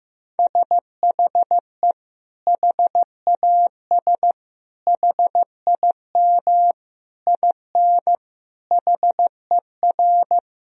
Morse code, heard as a keyed tone, about 15 wpm, 700 Hz. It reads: SHE HAS HIM IN HER
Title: Sentences from Top 1000 Words - Review Part 4 - 15wpm